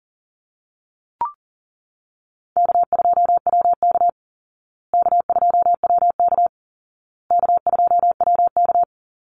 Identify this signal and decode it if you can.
Morse code, heard as a keyed tone, about 40 wpm, 700 Hz. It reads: K2WK K2WK K2WK
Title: Easy Call Signs Repeated 3x - 40wpm